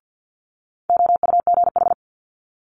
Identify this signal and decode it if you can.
Morse code, heard as a keyed tone, about 50 wpm, 700 Hz. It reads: OUGH
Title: Multi-letter Phonemes Quick Fire 50wpm